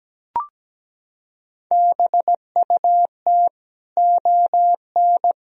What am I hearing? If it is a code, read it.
Morse code, heard as a keyed tone, about 17 wpm, 700 Hz. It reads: BUT ON
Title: Sentences from Top 1000 Words - Review Part 1 - 17wpm